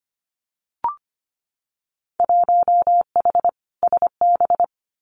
Morse code, heard as a keyed tone, about 25 wpm, 700 Hz. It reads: 1H SB
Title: Winter Field Day Exchanges 25wpm